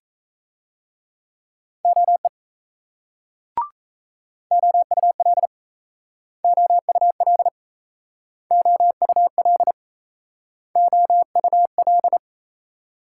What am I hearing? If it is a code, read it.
Morse code, heard as a keyed tone, about 38 wpm, 700 Hz. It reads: OE OUL OUL OUL OUL